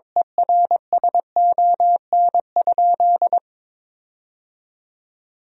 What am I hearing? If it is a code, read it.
Morse code, heard as a keyed tone, about 22 wpm, 700 Hz. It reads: ERSON?
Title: 2x Word-Spacing - Sentences from Top 400 Words - 22wpm